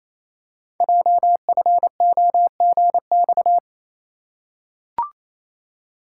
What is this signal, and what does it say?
Morse code, heard as a keyed tone, about 28 wpm, 700 Hz. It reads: JFOGX
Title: Code Groups - 5 Letters - 28wpm